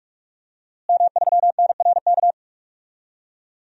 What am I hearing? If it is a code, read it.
Morse code, heard as a keyed tone, about 46 wpm, 700 Hz. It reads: M3NRK
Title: DXCC-100 Call Signs - 5 Characters - Repeated 3x - 45wpm